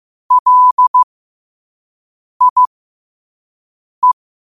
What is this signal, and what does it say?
Morse code, heard as a keyed tone, about 15 wpm, spaced as for 5 wpm, 1000 Hz. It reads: LIE
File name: qotdfiles / 5wpm